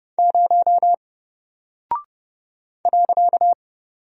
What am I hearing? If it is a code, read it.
Morse code, heard as a keyed tone, about 30 wpm, 700 Hz. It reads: 0 .
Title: Single Character (Letter:Number:Punctuation:Prosign) Alt 30wpm